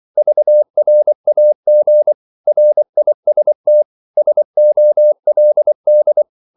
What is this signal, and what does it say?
Morse code, heard as a keyed tone, about 24 wpm, 600 Hz. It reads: VRAG RIST SOLD
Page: Korte danske ord 24wpm